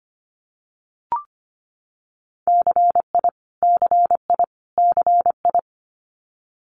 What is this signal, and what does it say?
Morse code, heard as a keyed tone, about 25 wpm, 700 Hz. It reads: CI CI CI
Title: Multi-letter Phonemes Repeated 3x 25wpm